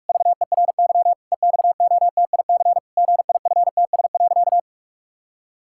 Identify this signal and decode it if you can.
Morse code, heard as a keyed tone, about 46 wpm, 700 Hz. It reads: VERY EXOTIC GIFTS.
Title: Pangram Challenge v2 - 45wpm